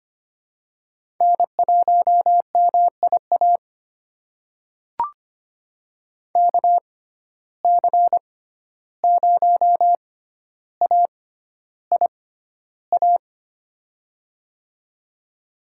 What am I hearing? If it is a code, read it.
Morse code, heard as a keyed tone, about 25 wpm, spaced as for 12 wpm, 700 Hz. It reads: N1MIA KC0AIA
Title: USA Call Signs - 6x Character Spacing - 25wpm